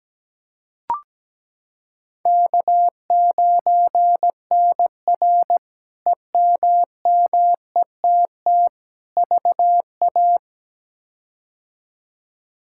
Morse code, heard as a keyed tone, about 17 wpm, 700 Hz. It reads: K9NR EMMETT VA
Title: K1USN SST QSO Practice - 17wpm